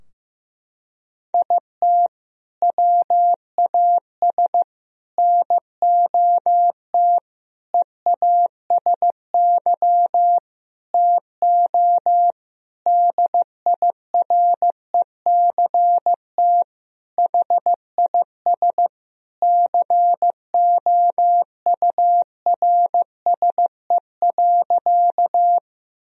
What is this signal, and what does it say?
Morse code, heard as a keyed tone, about 16 wpm, 700 Hz. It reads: IT WAS NOT EASY TO DIRECT HIS COURSE.
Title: Sentences from Top 1000 Words - Review Part 2 - 15wpm